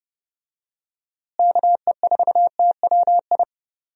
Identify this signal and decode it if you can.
Morse code, heard as a keyed tone, about 30 wpm, 700 Hz. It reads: KE4TWI
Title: USA Call Signs - 6 Characters - Repeated 3x - 30wpm